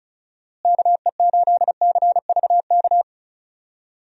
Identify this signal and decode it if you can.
Morse code, heard as a keyed tone, about 36 wpm, 700 Hz. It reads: KE8CVK
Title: USA Call Signs - 6 Characters - Repeated 2x - 35wpm